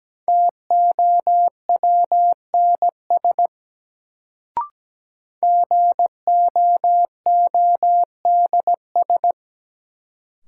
Morse code, heard as a keyed tone, about 17 wpm, 700 Hz. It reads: TOWNS GOODS
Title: Sentences from Top 1000 Words - Review Part 6 - New Word Drill - 17wpm